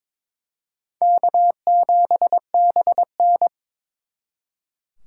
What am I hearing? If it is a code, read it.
Morse code, heard as a keyed tone, about 26 wpm, 700 Hz. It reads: K7BN
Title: Speed Racing - USA Call Signs 22wpm